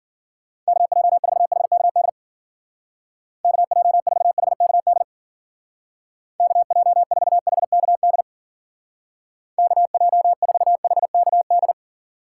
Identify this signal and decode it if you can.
Morse code, heard as a keyed tone, about 53 wpm, 700 Hz. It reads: KJ4HKD KJ4HKD KJ4HKD KJ4HKD